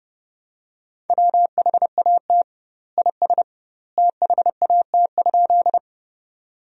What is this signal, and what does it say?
Morse code, heard as a keyed tone, about 32 wpm, 700 Hz. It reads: WHAT IS THAT?
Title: 2x Word-Spacing - Sentences from Top 100 Words - 30wpm